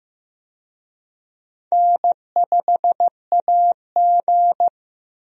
Morse code, heard as a keyed tone, about 15 wpm, 700 Hz. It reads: N5AG
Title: USA Call Signs - 4 Characters - Repeated 2x - 15wpm